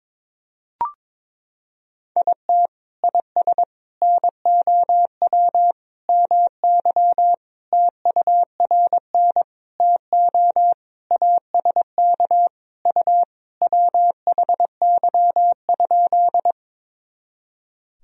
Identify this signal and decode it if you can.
Morse code, heard as a keyed tone, about 23 wpm, 700 Hz. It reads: IT IS NOW MY TURN TO ASK U WHY?
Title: Sentences from Top 200 Words and Abbreviations 22wpm